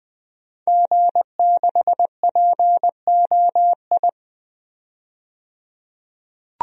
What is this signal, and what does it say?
Morse code, heard as a keyed tone, about 20 wpm, 700 Hz. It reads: G6POI
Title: DXCC-100 Call Signs - 5 Characters - 20wpm